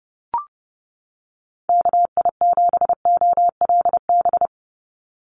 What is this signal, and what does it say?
Morse code, heard as a keyed tone, about 30 wpm, 700 Hz. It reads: KI7OLB
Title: USA Call Signs 30wpm